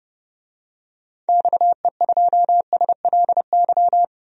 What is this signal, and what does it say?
Morse code, heard as a keyed tone, about 32 wpm, 700 Hz. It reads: XE2SLY
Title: DXCC-100 Call Signs - 6 Characters - Repeated 3x - 30wpm